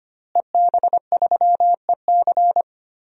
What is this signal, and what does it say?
Morse code, heard as a keyed tone, about 25 wpm, 700 Hz. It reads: EB3EC
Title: DXCC-100 Call Signs - 5 Characters - Repeated 2x - 25wpm